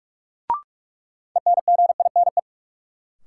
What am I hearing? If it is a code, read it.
Morse code, heard as a keyed tone, about 46 wpm, 700 Hz. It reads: ENGINE
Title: Sentences from Top 1500 Words - New Word Drill Alt - 45wpm